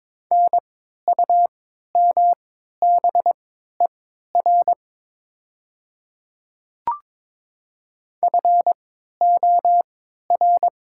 Morse code, heard as a keyed tone, about 22 wpm, spaced as for 12 wpm, 700 Hz. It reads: NUMBER FOR
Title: 3x Character Spacing - Top 100 Words - 22wpm